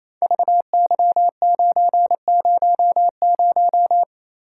Morse code, heard as a keyed tone, about 27 wpm, 700 Hz. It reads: VY900
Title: Code Groups - 5 Letter:Numbers - 28wpm